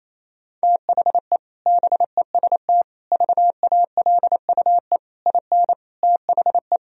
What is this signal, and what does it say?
Morse code, heard as a keyed tone, about 28 wpm, 700 Hz. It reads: THE BEST VALUE IN THE
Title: Sentences from Top 1000 Words - Review Part 3 - 28wpm